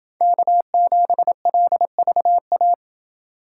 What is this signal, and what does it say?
Morse code, heard as a keyed tone, about 28 wpm, 700 Hz. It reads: K7LVA